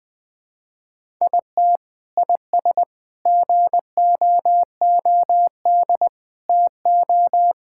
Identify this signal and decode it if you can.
Morse code, heard as a keyed tone, about 20 wpm, 700 Hz. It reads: IT IS GOOD TO
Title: Sentences with 3-Word Phrases - 20wpm